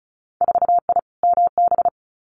Morse code, heard as a keyed tone, about 36 wpm, 700 Hz. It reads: 4I MB
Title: Winter Field Day Exchanges Encore 35wpm